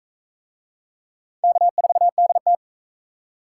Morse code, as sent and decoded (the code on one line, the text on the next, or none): -.- ....- -.. -
K4DT